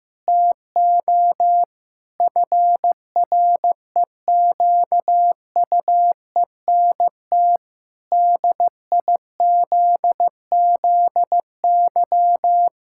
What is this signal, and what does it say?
Morse code, heard as a keyed tone, about 15 wpm, 700 Hz. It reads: TO FREQUENT DIZZY
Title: Pangram Challenge v2 - 15wpm